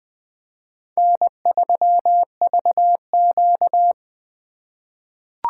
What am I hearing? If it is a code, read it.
Morse code, heard as a keyed tone, about 20 wpm, 700 Hz. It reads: N3VQ
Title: USA Call Signs - 4 Characters - 2x Character Spacing - 20wpm